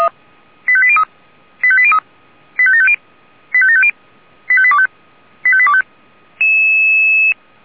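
Five-tone selective call (ZVEI), heard as quick runs of five tones, tones of about 70 ms. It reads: 85803 85803 85750 85750 85825 85825
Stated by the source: ZVEI five-tone selective call received by SDR.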